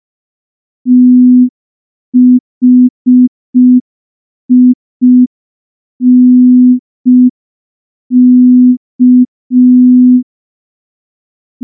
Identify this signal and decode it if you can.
Morse code, heard as a keyed tone, about 5 wpm, 250 Hz. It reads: THINK
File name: morse / morse-training-05WPM